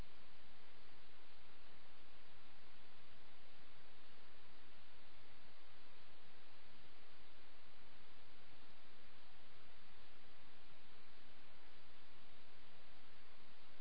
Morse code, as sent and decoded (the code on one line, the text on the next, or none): none